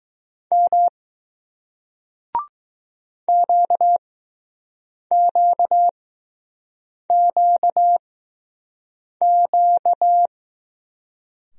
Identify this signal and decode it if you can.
Morse code, heard as a keyed tone, about 20 wpm, spaced as for 6 wpm, 700 Hz. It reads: M QQQQ